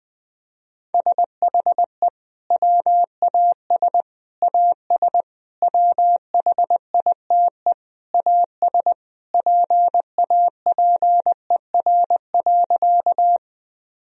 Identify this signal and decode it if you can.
Morse code, heard as a keyed tone, about 21 wpm, 700 Hz. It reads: SHE WAS AS WHITE AS PAPER.
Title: Sentences from Top 1000 Words - Review Part 1 - 20wpm